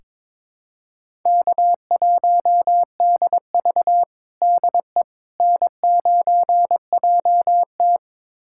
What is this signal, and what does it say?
Morse code, heard as a keyed tone, about 22 wpm, 700 Hz. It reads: K1DV DE N9JT
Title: USA Callsign DE Callsign - 4 Characters - 22wpm